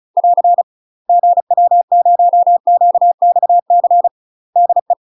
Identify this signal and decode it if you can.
Morse code, heard as a keyed tone, about 36 wpm, 700 Hz. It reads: <AR> GW0QXC DE
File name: UK MASTER QSOs TRAINING / 35WPM / UK MASTER QSOs 0035